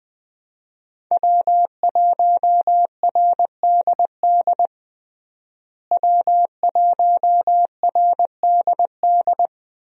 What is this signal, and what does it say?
Morse code, heard as a keyed tone, about 20 wpm, 700 Hz. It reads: W1RDD W1RDD
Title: USA Call Signs - 5 Characters - Repeated 3x - 20wpm